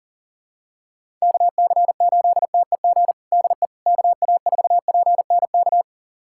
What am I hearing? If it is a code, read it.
Morse code, heard as a keyed tone, about 37 wpm, 700 Hz. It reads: KC8TEG DE KA4PNK
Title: USA Callsign DE Callsign 40wpm